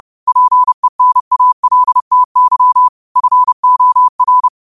Morse code, heard as a keyed tone, about 30 wpm, 1000 Hz. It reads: PENALTY FOR